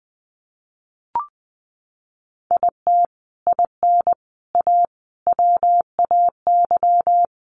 Morse code, heard as a keyed tone, about 20 wpm, 700 Hz. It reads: IT IN A WAY
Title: Sets of 5 Words Encore2 20wpm